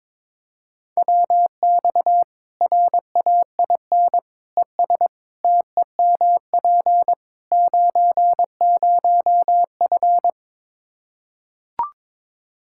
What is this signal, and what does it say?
Morse code, heard as a keyed tone, about 22 wpm, 700 Hz. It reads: WX RAIN ES TEMP 90F
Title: Practice QSO - Exchange 2 - Info - 22wpm